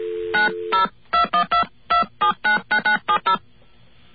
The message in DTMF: 902122799970